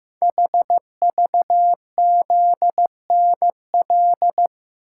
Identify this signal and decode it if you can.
Morse code, heard as a keyed tone, about 15 wpm, 700 Hz. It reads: HVZNL
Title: Code Groups - 5 Letters - 15wpm